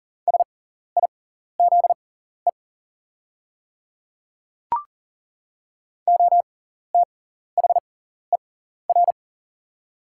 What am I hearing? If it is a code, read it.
Morse code, heard as a keyed tone, about 40 wpm, spaced as for 14 wpm, 690 Hz. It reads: SIZE OTHER
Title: 6x Character Spacing - Top 500 Words - 40wpm